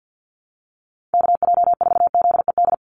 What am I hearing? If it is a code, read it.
Morse code, heard as a keyed tone, about 50 wpm, 700 Hz. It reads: KJ4ZED
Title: USA Call Signs 50wpm